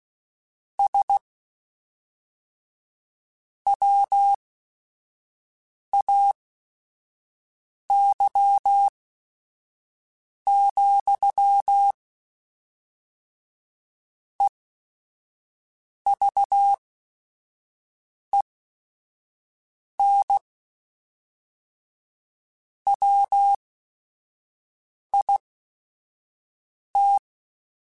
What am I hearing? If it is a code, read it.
Morse code, heard as a keyed tone, about 16 wpm, spaced as for 5 wpm, 800 Hz. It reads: SWAY, EVEN WIT
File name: Technical Subjects / Technical Subjects-mp3s / OR Gate @5